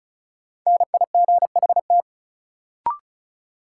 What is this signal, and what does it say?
Morse code, heard as a keyed tone, about 36 wpm, 700 Hz. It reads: NIGHT
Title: Sets of 4 Words - New Word Drill - 35wpm